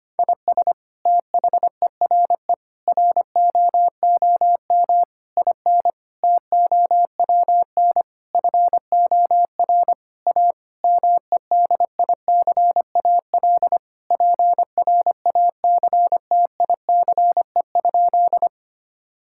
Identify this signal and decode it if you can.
Morse code, heard as a keyed tone, about 25 wpm, 700 Hz. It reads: IS THERE ROOM IN TOWN FOR A MEDICAL PRACTICE?
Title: Sentences from Top 1000 Words - Review Part 5 - 25wpm